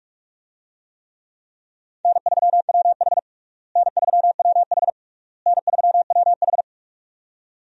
Morse code, heard as a keyed tone, about 42 wpm, 700 Hz. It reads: N3WH N3WH N3WH
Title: USA Call Signs - 4 Characters - Repeated 3x - 45wpm